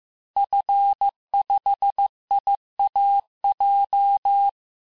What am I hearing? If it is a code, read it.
Morse code, heard as a keyed tone, about 15 wpm, 800 Hz. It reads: F5IAJ
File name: cours-du-lundi files / cours-1585-texte-FIN-de-QSO-vitesse-14mpm